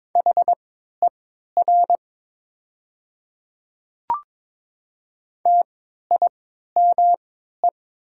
Morse code, heard as a keyed tone, about 22 wpm, spaced as for 12 wpm, 700 Hz. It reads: HER TIME
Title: 3x Character Spacing - Top 100 Words - 22wpm